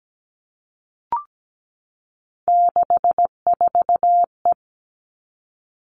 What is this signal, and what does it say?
Morse code, heard as a keyed tone, about 17 wpm, 700 Hz. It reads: 64E
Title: All Cut Numbers - 3-Digit 17wpm